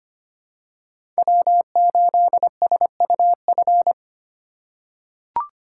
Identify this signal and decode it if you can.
Morse code, heard as a keyed tone, about 25 wpm, 700 Hz. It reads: W8SUF
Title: USA Call Signs - 5 Characters - 2x Character Spacing - 25wpm